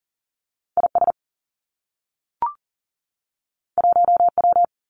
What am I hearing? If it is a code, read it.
Morse code, heard as a keyed tone, about 40 wpm, 700 Hz. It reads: IS 1W
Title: Two Challenging Letter-Number Combinations 40wpm